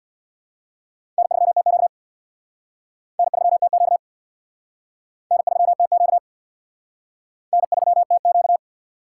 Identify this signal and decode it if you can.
Morse code, heard as a keyed tone, about 58 wpm, 700 Hz. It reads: N3TQ N3TQ N3TQ N3TQ
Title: Speed Racing - USA 1x2 Call Signs 50wpm